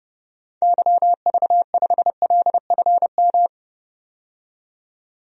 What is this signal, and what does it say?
Morse code, heard as a keyed tone, about 30 wpm, 700 Hz. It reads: YV5LFM
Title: DXCC-100 Call Signs - 6 Characters - 30wpm